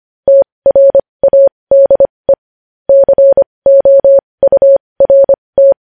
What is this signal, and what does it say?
Morse code, heard as a keyed tone, about 25 wpm, spaced as for 20 wpm, 550 Hz. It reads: TRADE COURT
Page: news-headlines-in-morse-code-at-20-wpm